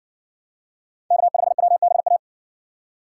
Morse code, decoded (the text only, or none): K5WBA